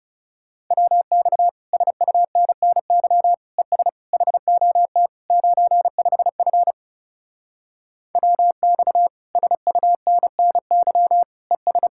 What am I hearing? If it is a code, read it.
Morse code, heard as a keyed tone, about 40 wpm, 700 Hz. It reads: WX SUNNY ES HOT 95F WX SUNNY ES